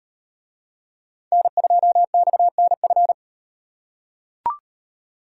Morse code, heard as a keyed tone, about 38 wpm, 700 Hz. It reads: N2XNF